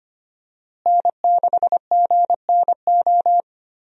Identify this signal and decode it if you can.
Morse code, heard as a keyed tone, about 25 wpm, 700 Hz. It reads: N6GNO
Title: USA Call Signs - 5 Characters - Repeated 3x - 25wpm